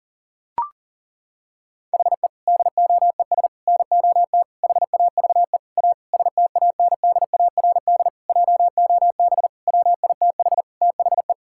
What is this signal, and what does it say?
Morse code, heard as a keyed tone, about 37 wpm, 700 Hz. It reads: HE DOES NOT HAVE A STANDARD JOB WITH THE
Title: Sentences from Top 1000 Words - Review Part 6 - 40wpm